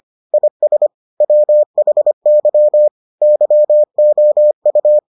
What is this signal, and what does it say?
Morse code, heard as a keyed tone, about 25 wpm, 600 Hz. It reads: IS WHY YOU